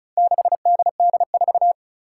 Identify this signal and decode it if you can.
Morse code, heard as a keyed tone, about 34 wpm, 700 Hz. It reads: 6DD4
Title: Four Challenging Letter-Number Combinations 35wpm